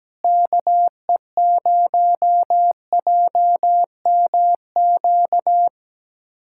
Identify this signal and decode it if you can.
Morse code, heard as a keyed tone, about 17 wpm, 700 Hz. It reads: KE0JMQ
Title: USA Callsign DE Callsign - 6 Characters - 17wpm